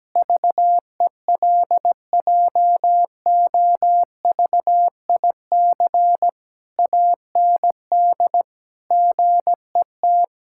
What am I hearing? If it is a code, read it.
Morse code, heard as a keyed tone, about 17 wpm, 700 Hz. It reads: VELJOVIC AND GET
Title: Pangram Challenge v2 - 17wpm